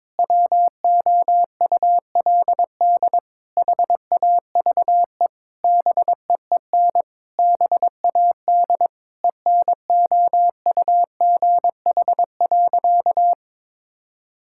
Morse code, heard as a keyed tone, about 22 wpm, 700 Hz. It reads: WOULD HAVE BEEN BAD ENOUGH.
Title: Sentences from Top 1000 Words - Review Part 5 - 22wpm